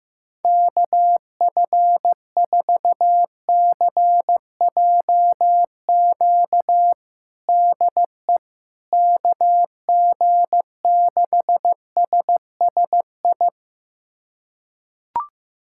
Morse code, heard as a keyed tone, about 15 wpm, 700 Hz. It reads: KF4CJQ DE KG6SSI
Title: USA Callsign DE Callsign - 6 Characters - 15wpm